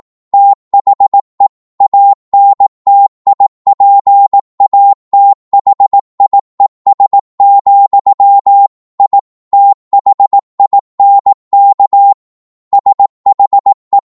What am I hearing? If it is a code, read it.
Morse code, heard as a keyed tone, about 18 wpm, 800 Hz. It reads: THE ANTIPATHIES, I THINK SHE